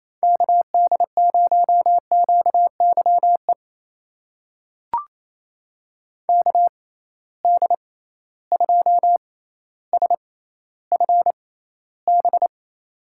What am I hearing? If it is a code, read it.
Morse code, heard as a keyed tone, about 28 wpm, spaced as for 14 wpm, 700 Hz. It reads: KD0QYE KD2SFB
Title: USA Call Signs - 6 Characters - 6x Character Spacing - 28wpm